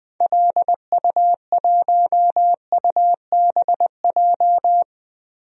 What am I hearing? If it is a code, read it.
Morse code, heard as a keyed tone, about 20 wpm, 700 Hz. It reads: LU1UBJ
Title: DXCC-100 Call Signs - 6 Characters - 20wpm